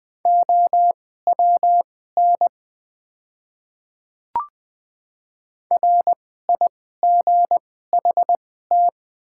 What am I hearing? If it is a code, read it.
Morse code, heard as a keyed tone, about 21 wpm, spaced as for 15 wpm, 700 Hz. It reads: OWN RIGHT
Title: 2x Character Spacing - Top 300 Words - 20wpm